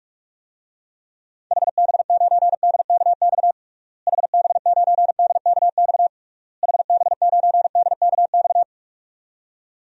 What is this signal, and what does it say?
Morse code, heard as a keyed tone, about 41 wpm, 700 Hz. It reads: HB9DKX HB9DKX HB9DKX
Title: DXCC-100 Call Signs - 6 Characters - Repeated 3x - 45wpm